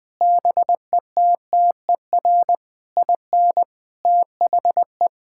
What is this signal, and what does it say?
Morse code, heard as a keyed tone, about 19 wpm, 700 Hz. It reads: BETTER IN THE
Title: Sentences with 6-Word Phrases - 20wpm